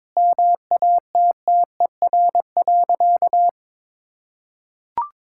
Morse code, heard as a keyed tone, about 22 wpm, 700 Hz. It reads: MATTER.
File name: practice / 22wpm / Sentences with 4-Word Phrases - 22wpm